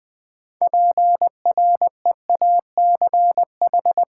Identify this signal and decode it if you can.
Morse code, heard as a keyed tone, about 20 wpm, 700 Hz. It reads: PREACH
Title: 6 Letter Words - First 3 Letters Given - 20wpm